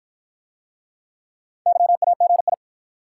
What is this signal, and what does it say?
Morse code, heard as a keyed tone, about 48 wpm, 700 Hz. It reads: YAGI